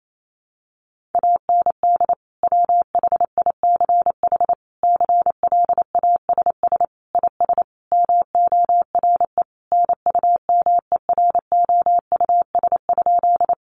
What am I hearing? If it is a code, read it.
Morse code, heard as a keyed tone, about 28 wpm, 700 Hz. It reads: AND WHICH CLASS IS MORE NUMEROUS?
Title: Sentences from Top 1400 Words 28wpm